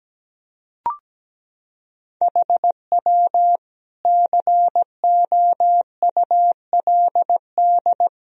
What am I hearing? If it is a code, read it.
Morse code, heard as a keyed tone, about 17 wpm, 700 Hz. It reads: HW COULD
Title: Sentences from Top 400 Words and Abbreviations 17wpm